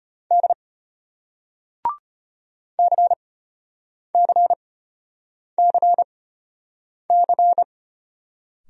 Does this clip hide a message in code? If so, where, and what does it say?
Yes: Morse code from 0.3 to 7.6 s, heard as a keyed tone, about 33 wpm, spaced as for 8 wpm, 700 Hz. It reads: D CCCC